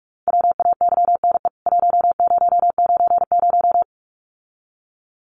Morse code, heard as a keyed tone, about 44 wpm, 700 Hz. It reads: WAYNE 1090
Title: CWT Exchange 45wpm